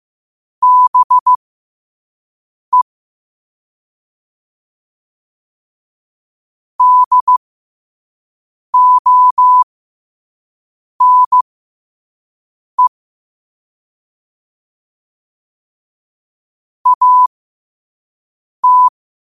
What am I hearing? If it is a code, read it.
Morse code, heard as a keyed tone, about 15 wpm, spaced as for 5 wpm, 1000 Hz. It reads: BE DONE AT
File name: qotdfiles / 5wpm